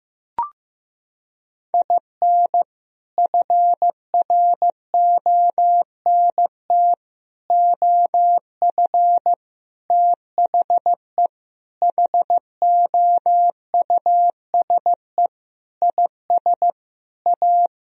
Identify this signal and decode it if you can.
Morse code, heard as a keyed tone, about 15 wpm, 700 Hz. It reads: IN FRONT OF THE HOUSE IS A
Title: Sentences from Top 1000 Words - Review Part 4 - 15wpm